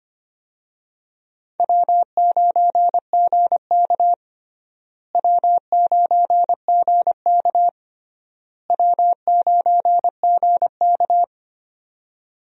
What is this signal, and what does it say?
Morse code, heard as a keyed tone, about 25 wpm, 700 Hz. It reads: W9GK W9GK W9GK
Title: USA Call Signs - 4 Characters - Repeated 3x - 25wpm